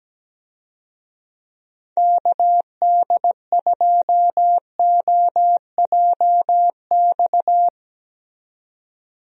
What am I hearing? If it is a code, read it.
Morse code, heard as a keyed tone, about 17 wpm, 700 Hz. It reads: KD2OJX
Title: USA Call Signs - 6 Characters - 17wpm